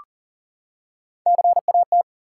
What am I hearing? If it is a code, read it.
Morse code, heard as a keyed tone, about 39 wpm, 700 Hz. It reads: CAT
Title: Three Letter Words - 40wpm